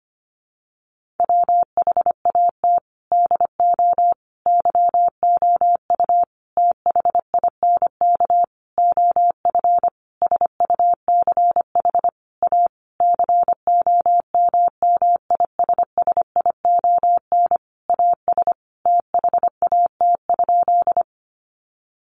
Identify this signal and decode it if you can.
Morse code, heard as a keyed tone, about 25 wpm, 700 Hz. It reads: WHAT DO YOU THINK OF SUCH A COMMISSION AS THAT?
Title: Sentences from Top 1400 Words 25wpm